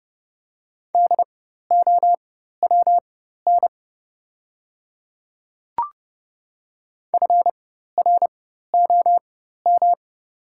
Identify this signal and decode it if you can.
Morse code, heard as a keyed tone, about 30 wpm, spaced as for 14 wpm, 700 Hz. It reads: DOWN FROM
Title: 4x Character Spacing - Top 100 Words - 30wpm